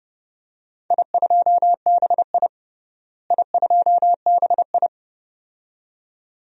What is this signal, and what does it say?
Morse code, heard as a keyed tone, about 29 wpm, 700 Hz. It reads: I2BI I2BI
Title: DXCC-100 Call Signs - 4 Characters - Repeated 2x - 30wpm